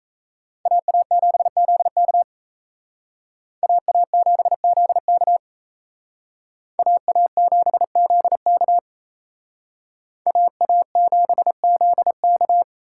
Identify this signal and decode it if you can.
Morse code, heard as a keyed tone, about 35 wpm, 700 Hz. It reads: AA7ZK AA7ZK AA7ZK AA7ZK